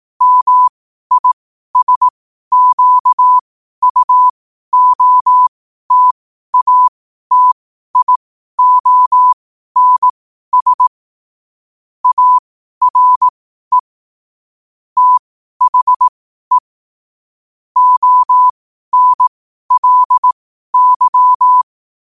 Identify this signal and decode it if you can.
Morse code, heard as a keyed tone, about 18 wpm, spaced as for 12 wpm, 1000 Hz. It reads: MISQUOTATIONS ARE THE ONLY
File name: qotdfiles / 12wpm